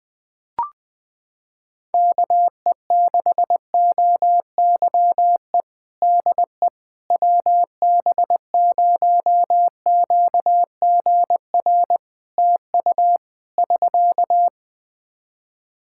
Morse code, heard as a keyed tone, about 20 wpm, 700 Hz. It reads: KE6OYE DE WB0QGR TU <SK>
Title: Practice QSO - Exchange 4 - Ending - 20wpm